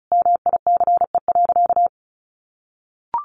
Morse code, heard as a keyed tone, about 34 wpm, 700 Hz. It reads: MICE.
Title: A Sound Of Thunder 35wpm